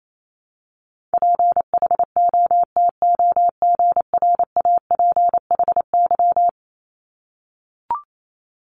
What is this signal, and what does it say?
Morse code, heard as a keyed tone, about 28 wpm, 700 Hz. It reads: PHOTOGRAPHY
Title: Multi-letter Phonemes 28wpm